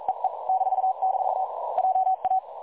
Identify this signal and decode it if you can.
Morse code, heard as a keyed tone, about 42 wpm, 760 Hz. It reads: EE -- JA
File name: dlazaawansowanych / nagraniazpasma / websdr recording 3565.0kHz